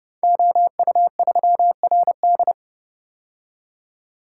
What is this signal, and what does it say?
Morse code, heard as a keyed tone, about 30 wpm, 700 Hz. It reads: OU3RD
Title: Code Groups - 5 Letter:Numbers - 30wpm